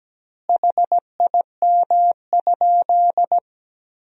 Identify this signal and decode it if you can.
Morse code, heard as a keyed tone, about 17 wpm, 700 Hz. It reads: HIM?
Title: 2x Word-Spacing - Sentences from Top 100 Words - 17wpm